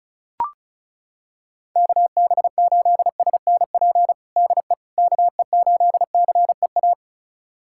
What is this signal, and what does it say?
Morse code, heard as a keyed tone, about 36 wpm, 700 Hz. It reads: KB8SNP DE KE8CEA
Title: USA Callsign DE Callsign - 6 Characters - 35wpm